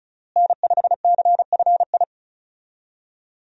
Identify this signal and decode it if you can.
Morse code, heard as a keyed tone, about 36 wpm, 700 Hz. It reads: N5CFI
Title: USA Call Signs - 5 Characters - Repeated 3x - 35wpm